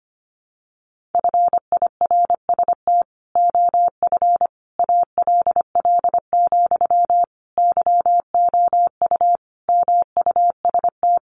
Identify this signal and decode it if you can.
Morse code, heard as a keyed tone, about 25 wpm, 700 Hz. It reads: FIRST OF ALL, YOU MUST
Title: Sentences from Top 1500 Words 25wpm